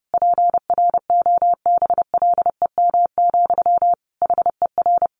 Morse code, heard as a keyed tone, about 30 wpm, 700 Hz. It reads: PROBLEM, HER
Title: Sentences from Top 1200 Words 30wpm